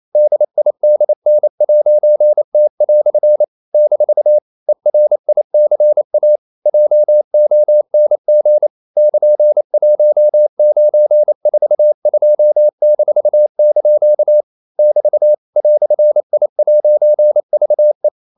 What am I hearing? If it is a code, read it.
Morse code, heard as a keyed tone, about 28 wpm, 600 Hz. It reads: DIDN'T" - ERICA JONG <KN>1942-) <BT> "I'VE
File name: CW / Practice / EnglishQuotations0228